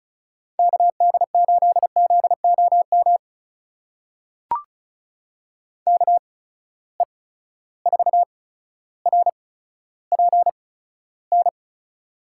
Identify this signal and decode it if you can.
Morse code, heard as a keyed tone, about 36 wpm, spaced as for 15 wpm, 700 Hz. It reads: KD8ZOM KE4RPN
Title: USA Call Signs - 6 Characters - 8x Character Spacing - 35wpm